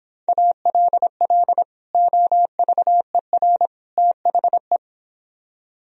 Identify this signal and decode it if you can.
Morse code, heard as a keyed tone, about 25 wpm, 700 Hz. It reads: ALL OVER THE